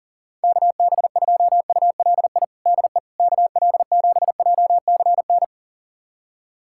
Morse code, heard as a keyed tone, about 40 wpm, 700 Hz. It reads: KB2ULI DE KL7JCN
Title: USA Callsign DE Callsign 40wpm